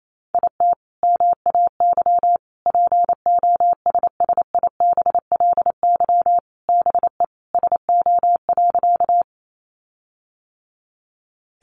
Morse code, heard as a keyed tone, about 28 wpm, 700 Hz. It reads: IT MAY POSSIBLY BE SO.
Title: Sentences from Top 1300 Words 28wpm